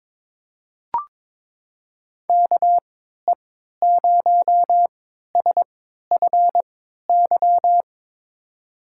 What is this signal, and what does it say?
Morse code, heard as a keyed tone, about 22 wpm, spaced as for 13 wpm, 700 Hz. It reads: KE0SFY
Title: USA Call Signs - 6 Characters - 3x Character Spacing - 22wpm